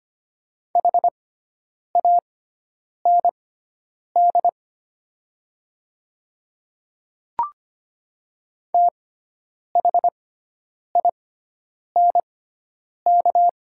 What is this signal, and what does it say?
Morse code, heard as a keyed tone, about 25 wpm, spaced as for 9 wpm, 700 Hz. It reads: HAND THINK
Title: 6x Character Spacing - Top 300 Words - 25wpm